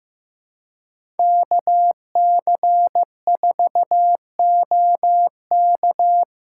Morse code, heard as a keyed tone, about 15 wpm, 700 Hz. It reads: KC4OK
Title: USA Call Signs - 5 Characters - 3x Character Spacing - 15wpm